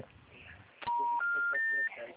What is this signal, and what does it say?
Special information tone, with speech.
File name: out- -2011- - - .56